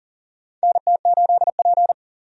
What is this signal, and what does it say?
Morse code, heard as a keyed tone, about 37 wpm, 700 Hz. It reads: NT8P
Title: USA Call Signs - 4 Characters - 40wpm